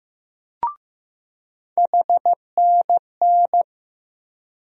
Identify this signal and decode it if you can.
Morse code, heard as a keyed tone, about 15 wpm, 700 Hz. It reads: HNN
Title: USA Call Sign Postfixes - 3 Letters 15wpm